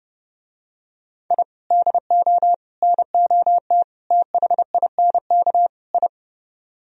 Morse code, heard as a keyed tone, about 30 wpm, 700 Hz. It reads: I DO NOT THINK I